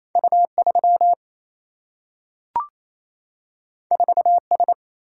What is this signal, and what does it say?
Morse code, heard as a keyed tone, about 28 wpm, 700 Hz. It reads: U3 4S
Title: Single-Letter Single-Number 28wpm